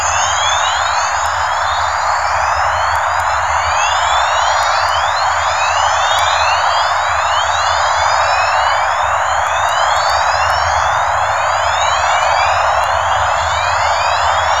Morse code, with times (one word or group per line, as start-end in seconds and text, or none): none